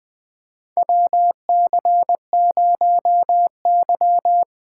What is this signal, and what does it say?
Morse code, heard as a keyed tone, about 20 wpm, 700 Hz. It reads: WC0Y
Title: 2023 SOTA Chaser Call Signs - 20wpm